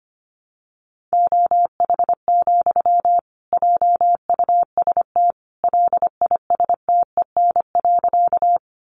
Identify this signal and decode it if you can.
Morse code, heard as a keyed tone, about 25 wpm, 700 Hz. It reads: OH, JUST LISTEN.
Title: Sentences from Top 500 Words 25wpm